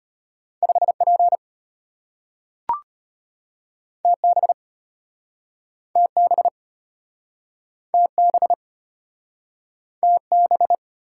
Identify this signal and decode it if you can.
Morse code, heard as a keyed tone, about 35 wpm, 700 Hz. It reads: 5P TB TB TB TB